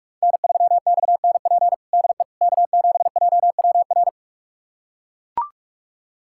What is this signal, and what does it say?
Morse code, heard as a keyed tone, about 41 wpm, 700 Hz. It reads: N3XNP DE K7JWR
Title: USA Callsign DE Callsign - 5 Characters - 45wpm